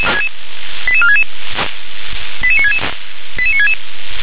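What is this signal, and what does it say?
Five-tone selective call (ZVEI), heard as quick runs of five tones, tones of about 80 ms. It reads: 88466 88966 88966
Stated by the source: ZVEI five-tone selective call received on SDR.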